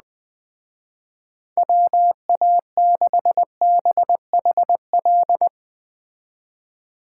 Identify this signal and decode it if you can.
Morse code, heard as a keyed tone, about 19 wpm, 700 Hz. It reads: WA6BHL
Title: Speed Racing - USA 2x3 Call Signs 17wpm